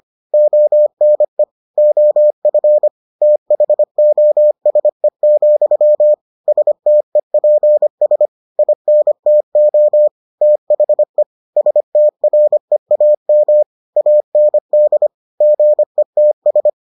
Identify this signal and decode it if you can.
Morse code, heard as a keyed tone, about 25 wpm, 600 Hz. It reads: ONE OF THOSE, STEPS INTO THE STREAM AND GETS